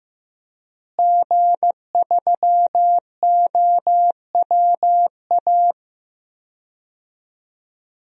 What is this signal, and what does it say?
Morse code, heard as a keyed tone, about 15 wpm, 700 Hz. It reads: G3OWA
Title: DXCC-100 Call Signs - 5 Characters - Repeated 3x - 15wpm